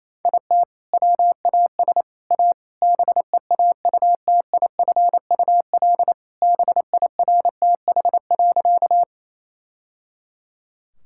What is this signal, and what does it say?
Morse code, heard as a keyed tone, about 28 wpm, 700 Hz. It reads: IT WAS A BEAUTIFUL BIRTH.
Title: Sentences from Top 1000 Words - Review Part 5 - 28wpm